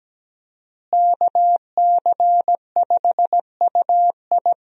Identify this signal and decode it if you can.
Morse code, heard as a keyed tone, about 17 wpm, 700 Hz. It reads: KC5UI
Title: USA Call Signs - 5 Characters - Repeated 3x - 17wpm